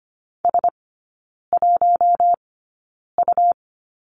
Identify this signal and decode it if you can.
Morse code, heard as a keyed tone, about 25 wpm, spaced as for 9 wpm, 700 Hz. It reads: S1U
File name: course / 25:10wpm / 100 Lesson 17e - Letter u - Three Characters 25wpm